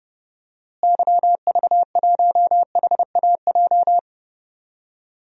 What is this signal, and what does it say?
Morse code, heard as a keyed tone, about 31 wpm, 700 Hz. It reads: YV1HAJ
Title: DXCC-100 Call Signs - 6 Characters - Repeated 2x - 30wpm